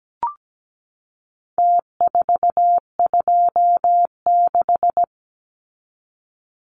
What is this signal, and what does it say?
Morse code, heard as a keyed tone, about 17 wpm, 700 Hz. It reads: T426
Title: Numbers - Contest Style - 4-Digit 17wpm